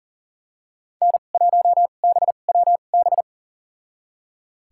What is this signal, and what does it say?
Morse code, heard as a keyed tone, about 40 wpm, spaced as for 29 wpm, 700 Hz. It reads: N1BWB
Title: USA Call Signs - 5 Characters - 2x Character Spacing - 40wpm